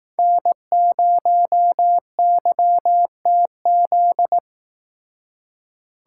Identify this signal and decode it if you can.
Morse code, heard as a keyed tone, about 18 wpm, 700 Hz. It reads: N0YTZ
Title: Speed Racing - USA Call Signs 15wpm